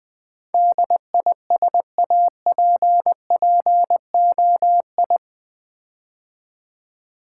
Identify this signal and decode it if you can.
Morse code, heard as a keyed tone, about 20 wpm, 700 Hz. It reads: DISAPPOI
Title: Build A Word - 4 Letters At A Time - 12-Letter Words - 20wpm